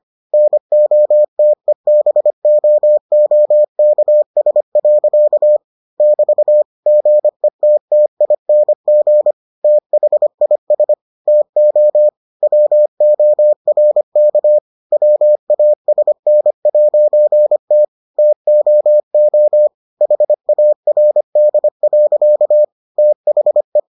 Morse code, heard as a keyed tone, about 25 wpm, 600 Hz. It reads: NOTEBOOKS. <BT> GETTING THIS TO WORK WASN'T TOO HARD. THE